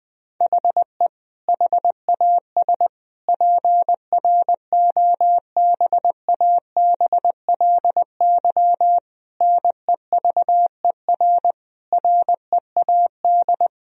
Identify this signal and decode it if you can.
Morse code, heard as a keyed tone, about 21 wpm, 700 Hz. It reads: HE HAS PROBABLY NEVER READ
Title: Sentences from Top 1000 Words - Review Part 4 - 20wpm